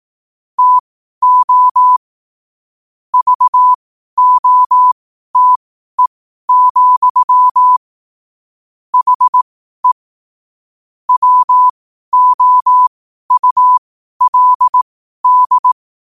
Morse code, heard as a keyed tone, about 18 wpm, spaced as for 12 wpm, 1000 Hz. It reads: TO VOTE, HE WOULD
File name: qotdfiles / 12wpm